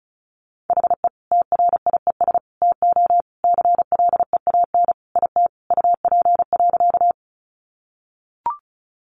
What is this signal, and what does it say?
Morse code, heard as a keyed tone, about 34 wpm, 700 Hz. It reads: HE TRIES TO CLEAN IT UP.
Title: Additional Easy Phrases 35wpm